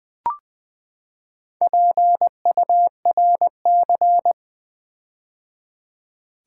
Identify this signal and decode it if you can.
Morse code, heard as a keyed tone, about 20 wpm, 700 Hz. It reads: PURC
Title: Build A Word - 4 Letters At A Time - 8-Letter Words - 20wpm